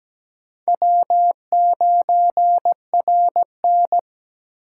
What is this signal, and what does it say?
Morse code, heard as a keyed tone, about 20 wpm, 700 Hz. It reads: W9RN
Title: Speed Racing - USA 1x2 Call Signs 17wpm